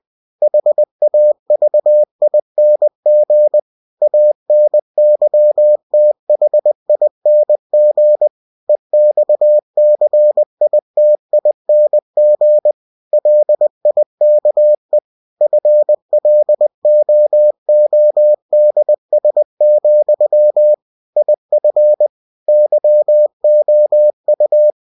Morse code, heard as a keyed tone, about 20 wpm, 600 Hz. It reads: HAVING ANYTHING EXCITING LIKE FLOODS, IF YOU
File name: cw-books / winnie-the-pooh / winnie-20wpm-page-0091